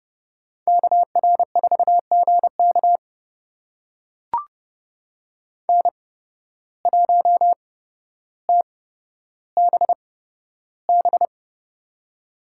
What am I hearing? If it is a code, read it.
Morse code, heard as a keyed tone, about 30 wpm, spaced as for 13 wpm, 700 Hz. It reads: KR4GK N1TBB
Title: USA Call Signs - 5 Characters - 8x Character Spacing - 30wpm